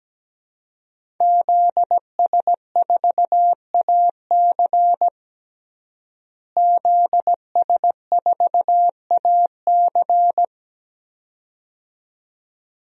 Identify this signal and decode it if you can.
Morse code, heard as a keyed tone, about 17 wpm, 700 Hz. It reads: ZS4AC ZS4AC
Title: DXCC-100 Call Signs - 5 Characters - Repeated 2x - 17wpm